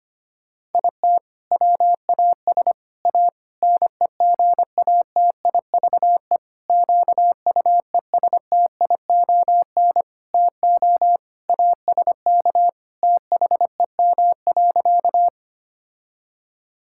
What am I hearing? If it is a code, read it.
Morse code, heard as a keyed tone, about 25 wpm, 700 Hz. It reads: IT WAS A NEGATIVE QUESTION TO ASK THEM.
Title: Sentences from Top 1000 Words - Review Part 6 - 25wpm